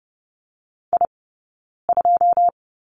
Morse code, heard as a keyed tone, about 30 wpm, 700 Hz. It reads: I 2
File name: Lesson 19d - Letter w - Two Characters 30wpm